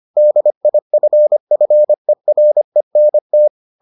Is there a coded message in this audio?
Yes: Morse code from 0.2 to 3.5 s, heard as a keyed tone, about 25 wpm, 600 Hz. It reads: DIFFERENT